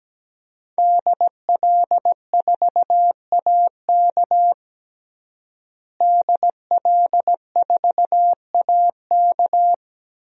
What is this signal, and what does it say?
Morse code, heard as a keyed tone, about 17 wpm, 700 Hz. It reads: DL4AK DL4AK
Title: DXCC-100 Call Signs - 5 Characters - Repeated 2x - 17wpm